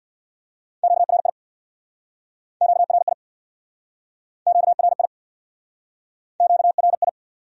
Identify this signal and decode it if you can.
Morse code, heard as a keyed tone, about 64 wpm, 700 Hz. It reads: QRI QRI QRI QRI